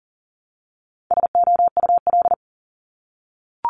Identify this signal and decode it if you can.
Morse code, heard as a keyed tone, about 42 wpm, 700 Hz. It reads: SOUL
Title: Multi-letter Phonemes 40wpm